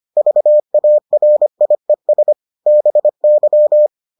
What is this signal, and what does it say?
Morse code, heard as a keyed tone, about 25 wpm, 600 Hz. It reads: VARIES BY